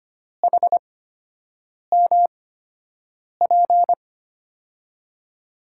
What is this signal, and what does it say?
Morse code, heard as a keyed tone, about 25 wpm, spaced as for 7 wpm, 700 Hz. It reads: HMP
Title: USA Call Signs - 8x Character Spacing - 25wpm